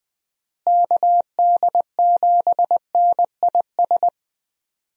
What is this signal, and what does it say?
Morse code, heard as a keyed tone, about 20 wpm, 700 Hz. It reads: KD7NIS
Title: USA Call Signs - 6 Characters - Repeated 2x - 20wpm